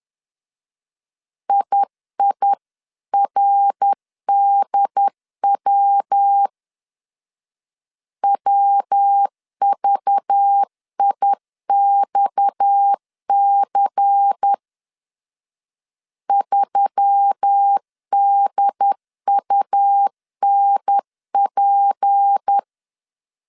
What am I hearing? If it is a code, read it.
Morse code, heard as a keyed tone, about 10 wpm, 800 Hz. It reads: IIRDW WVIXC 3DUNP